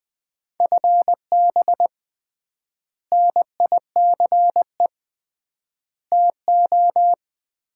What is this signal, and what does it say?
Morse code, heard as a keyed tone, about 20 wpm, 700 Hz. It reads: FB NICE TO
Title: Common QSO Phrases - 3x Word-Spacing - 20wpm